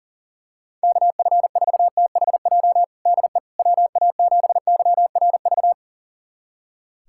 Morse code, heard as a keyed tone, about 40 wpm, 700 Hz. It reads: KF4THJ DE WA7YRV
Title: USA Callsign DE Callsign 40wpm